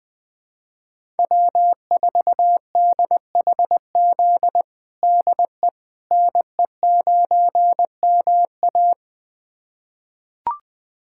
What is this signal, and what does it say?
Morse code, heard as a keyed tone, about 20 wpm, 700 Hz. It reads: W4DHZ DE NE9MA
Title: USA Callsign DE Callsign - 5 Characters - 20wpm